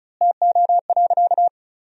Morse code, heard as a keyed tone, about 34 wpm, 700 Hz. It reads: TO.
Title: 4x Word-Spacing - Sentences from Top 400 Words - 35wpm